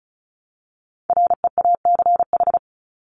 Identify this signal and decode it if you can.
Morse code, heard as a keyed tone, about 34 wpm, 700 Hz. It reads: REACH
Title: Top 500 Words Repeated 35wpm